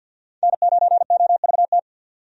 Morse code, heard as a keyed tone, about 46 wpm, 700 Hz. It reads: N9OVT
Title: USA Call Signs - 5 Characters - 50wpm